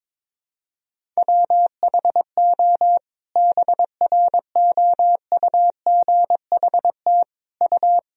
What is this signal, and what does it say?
Morse code, heard as a keyed tone, about 22 wpm, 700 Hz. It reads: WHO BROUGHT U
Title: Sentences from Top 500 Words and Abbreviations 22wpm